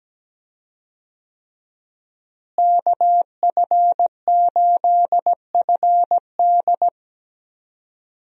Morse code, heard as a keyed tone, about 17 wpm, 700 Hz. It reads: KF8FD
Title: USA Call Signs - 5 Characters - Repeated 3x - 17wpm